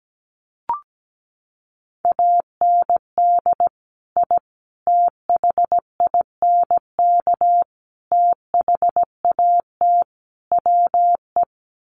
Morse code, heard as a keyed tone, about 17 wpm, 700 Hz. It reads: AND I THINK THAT WE
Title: Sets of 5 Words Encore2 17wpm